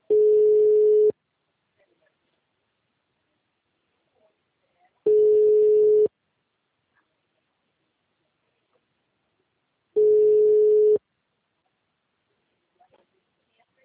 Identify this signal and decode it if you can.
Ringback tone (Europe, 425 Hz), with speech.